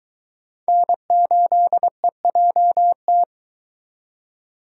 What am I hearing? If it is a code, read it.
Morse code, heard as a keyed tone, about 23 wpm, 700 Hz. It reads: N8EJT